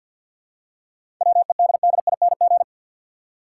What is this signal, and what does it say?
Morse code, heard as a keyed tone, about 52 wpm, 700 Hz. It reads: WEDDING
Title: Seven Letter Words - 50wpm